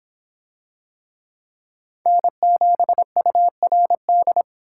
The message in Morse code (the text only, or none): N7URD